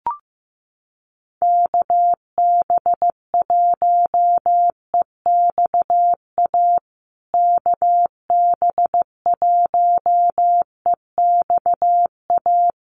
Morse code, heard as a keyed tone, about 15 wpm, 700 Hz. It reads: KB1EXA KB1EXA
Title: USA Call Signs Repeated 3x 15wpm